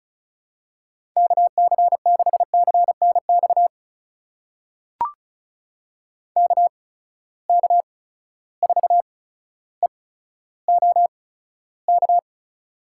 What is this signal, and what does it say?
Morse code, heard as a keyed tone, about 36 wpm, spaced as for 15 wpm, 700 Hz. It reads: KC6CNX KK4EOK
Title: USA Call Signs - 6 Characters - 8x Character Spacing - 35wpm